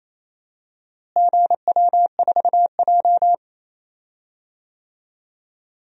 Morse code, heard as a keyed tone, about 27 wpm, 700 Hz. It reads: GW4J
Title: DXCC-100 Call Signs - 4 Characters - Repeated 2x - 28wpm